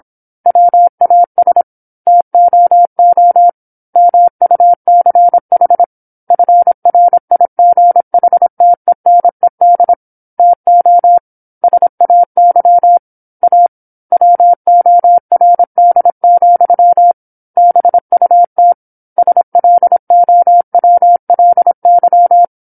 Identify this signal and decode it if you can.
Morse code, heard as a keyed tone, about 27 wpm, 700 Hz. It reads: WAS TOO MUCH FRIGHTENED TO SAY A WORD, BUT SLOWLY
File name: 08-Chapter-08 26 WPM